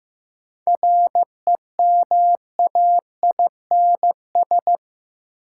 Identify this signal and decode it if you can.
Morse code, heard as a keyed tone, about 15 wpm, 700 Hz. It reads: REMAINS
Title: Sets of 5 Words Encore - New Word Drill - 15wpm